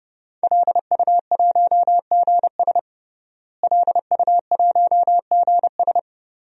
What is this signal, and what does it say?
Morse code, heard as a keyed tone, about 31 wpm, 700 Hz. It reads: LU1GS LU1GS
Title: DXCC-100 Call Signs - 5 Characters - Repeated 3x - 30wpm